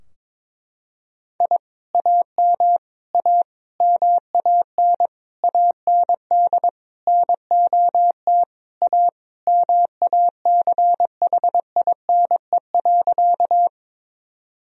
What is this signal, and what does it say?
Morse code, heard as a keyed tone, about 22 wpm, 700 Hz. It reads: I AM A MAN AND NOT A MACHINE.
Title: Sentences from Top 1000 Words - Review Part 2 - 22wpm